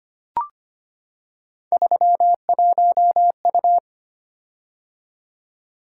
Morse code, heard as a keyed tone, about 25 wpm, 700 Hz. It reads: 31U
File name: Three Letter-Number Combinations - 25wpm